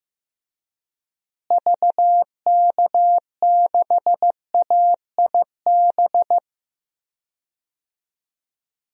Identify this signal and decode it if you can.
Morse code, heard as a keyed tone, about 16 wpm, 700 Hz. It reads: VK6AIB
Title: DXCC-100 Call Signs - 6 Characters - Repeated 2x - 15wpm